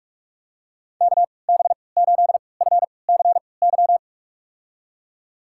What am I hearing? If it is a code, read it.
Morse code, heard as a keyed tone, about 46 wpm, spaced as for 26 wpm, 700 Hz. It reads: KB8FCY
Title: USA Call Signs - 6 Characters - 3x Character Spacing - 45wpm